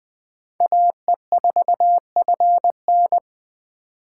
Morse code, heard as a keyed tone, about 20 wpm, 700 Hz. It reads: AE4FN
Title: Speed Racing - USA 2x2 Call Signs 17wpm